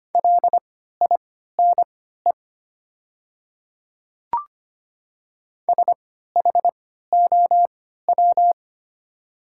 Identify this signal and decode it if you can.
Morse code, heard as a keyed tone, about 25 wpm, spaced as for 14 wpm, 700 Hz. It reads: LINE SHOW
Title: 3x Character Spacing - Top 200 Words - 25wpm